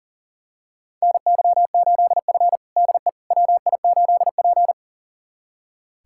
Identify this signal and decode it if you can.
Morse code, heard as a keyed tone, about 37 wpm, 700 Hz. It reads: NY8F DE WI8P
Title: USA Callsign DE Callsign - 4 Characters - 40wpm